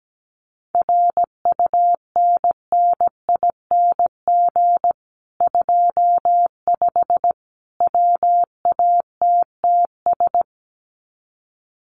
Morse code, heard as a keyed tone, about 17 wpm, 700 Hz. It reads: RUNNING 25 WATTS
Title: Common QSO Phrases 17wpm